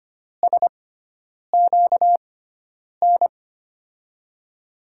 Morse code, heard as a keyed tone, about 25 wpm, spaced as for 9 wpm, 700 Hz. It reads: SQN
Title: USA Call Signs - 6x Character Spacing - 25wpm